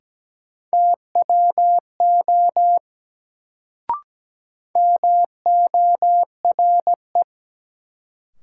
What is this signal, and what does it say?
Morse code, heard as a keyed tone, about 17 wpm, 700 Hz. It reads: TWO MORE
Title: Sets of 2 Words - New Word Drill - 17wpm